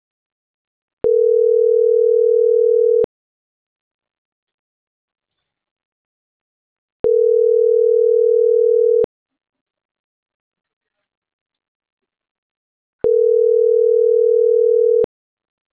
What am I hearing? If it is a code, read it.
Ringback tone (North America), 440 + 480 Hz.